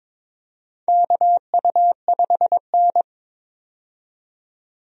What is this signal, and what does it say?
Morse code, heard as a keyed tone, about 23 wpm, 700 Hz. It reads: KU5N
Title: USA Call Signs - 4 Characters - 22wpm